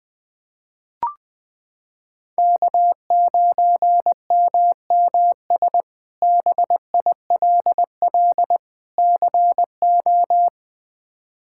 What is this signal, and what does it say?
Morse code, heard as a keyed tone, about 20 wpm, 700 Hz. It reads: K9MMS BILL CO
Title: K1USN SST QSO Practice - 20wpm